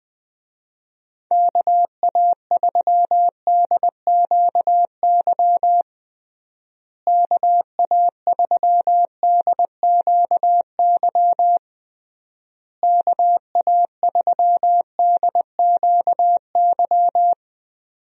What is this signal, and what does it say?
Morse code, heard as a keyed tone, about 20 wpm, 700 Hz. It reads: KA3DQY KA3DQY KA3DQY
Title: USA Call Signs - 6 Characters - Repeated 3x - 20wpm